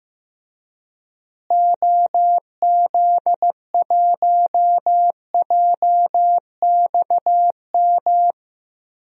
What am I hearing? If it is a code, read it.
Morse code, heard as a keyed tone, about 15 wpm, 700 Hz. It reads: OZ1JXM
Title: DXCC-100 Call Signs - 6 Characters - Repeated 2x - 15wpm